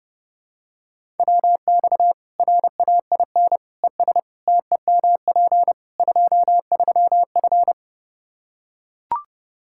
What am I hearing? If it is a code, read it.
Morse code, heard as a keyed tone, about 30 wpm, 700 Hz. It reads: WX RAIN ES TEMP 23F
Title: Practice QSO - Exchange 2 - Info - 30wpm